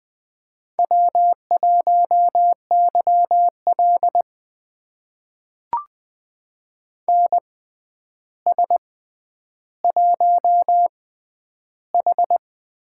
Morse code, heard as a keyed tone, about 20 wpm, spaced as for 10 wpm, 700 Hz. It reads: W1YL NS1H
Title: USA Call Signs - 4 Characters - 6x Character Spacing - 20wpm